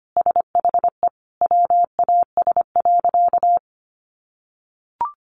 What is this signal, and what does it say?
Morse code, heard as a keyed tone, about 25 wpm, 700 Hz. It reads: SHE WAS.
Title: Sentences from Top 900 Words 25wpm